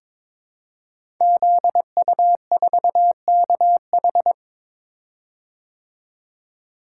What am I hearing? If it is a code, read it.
Morse code, heard as a keyed tone, about 22 wpm, 700 Hz. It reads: ZU4KH
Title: Code Groups - 5 Letter:Numbers - 22wpm